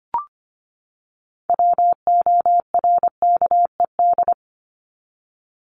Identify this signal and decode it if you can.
Morse code, heard as a keyed tone, about 25 wpm, 700 Hz. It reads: WORKED
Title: ED Suffix - Top 100 Words with Base Words 25wpm